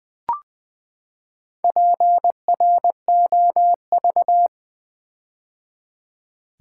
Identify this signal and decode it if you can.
Morse code, heard as a keyed tone, about 20 wpm, 700 Hz. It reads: PROV
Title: Build A Word - 4 Letters At A Time - 8-Letter Words - 20wpm